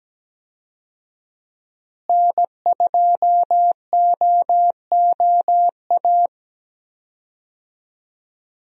Morse code, heard as a keyed tone, about 17 wpm, 700 Hz. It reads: N2OOA
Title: USA Call Signs - 5 Characters - Repeated 2x - 17wpm